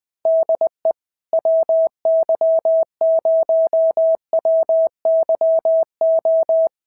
Morse code, heard as a keyed tone, about 20 wpm, 650 Hz. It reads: DE WY0WYO
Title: SOTA QSO Practice 20wpm